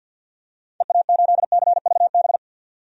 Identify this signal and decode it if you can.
Morse code, heard as a keyed tone, about 46 wpm, 700 Hz. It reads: EA8CVB
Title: DXCC-100 Call Signs - 6 Characters - 50wpm